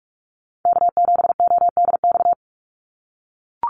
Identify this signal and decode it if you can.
Morse code, heard as a keyed tone, about 44 wpm, 690 Hz. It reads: K7ODX
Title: NA SOTA Chaser Callsigns 45wpm